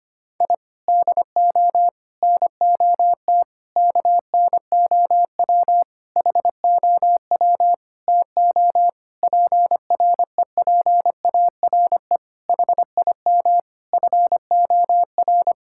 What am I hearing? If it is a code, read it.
Morse code, heard as a keyed tone, about 25 wpm, 700 Hz. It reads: I DO NOT KNOW HOW TO PREPARE HIM FOR
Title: Sentences from Top 1000 Words - Review Part 6 - 25wpm